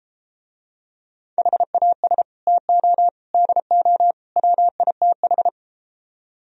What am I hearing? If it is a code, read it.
Morse code, heard as a keyed tone, about 33 wpm, 700 Hz. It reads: HAS TO DO WITH